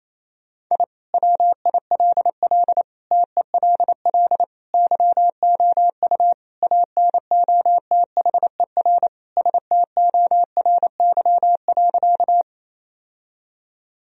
Morse code, heard as a keyed tone, about 28 wpm, 700 Hz. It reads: I WILL TELL YOU ANOTHER STORY.
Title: Sentences from Top 1000 Words - Review Part 1 - 28wpm